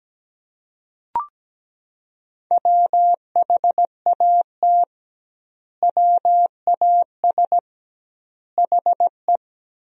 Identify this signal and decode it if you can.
Morse code, heard as a keyed tone, about 17 wpm, 700 Hz. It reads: WHAT WAS HE
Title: 2x Word-Spacing - Sentences from Top 100 Words - 17wpm